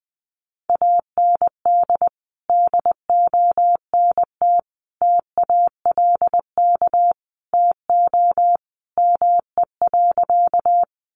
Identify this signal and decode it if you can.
Morse code, heard as a keyed tone, about 19 wpm, 700 Hz. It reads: AND DONT TALK TO ME.
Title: Sentences from Top 400 Words 20wpm